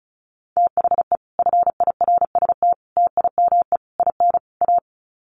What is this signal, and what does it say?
Morse code, heard as a keyed tone, about 34 wpm, 700 Hz. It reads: THE FIRST TIME IN A
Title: Sets of 5 Words Encore 35wpm